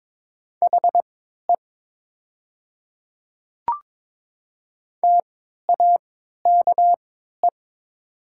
Morse code, heard as a keyed tone, about 22 wpm, spaced as for 13 wpm, 700 Hz. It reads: HE TAKE
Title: 3x Character Spacing - Top 200 Words - 22wpm